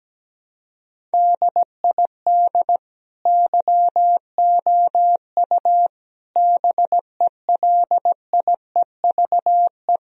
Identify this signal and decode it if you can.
Morse code, heard as a keyed tone, about 17 wpm, 700 Hz. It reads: DID YOU BELIEVE
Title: Sentences from Top 1000 Words - Review Part 1 - 17wpm